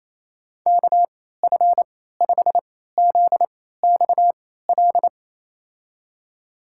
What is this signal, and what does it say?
Morse code, heard as a keyed tone, about 27 wpm, spaced as for 16 wpm, 700 Hz. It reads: KF5ZXL
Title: USA Call Signs - 6 Characters - 3x Character Spacing - 28wpm